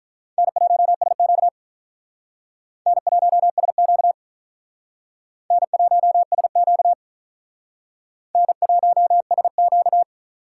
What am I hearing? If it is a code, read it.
Morse code, heard as a keyed tone, about 46 wpm, 700 Hz. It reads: N1SQ N1SQ N1SQ N1SQ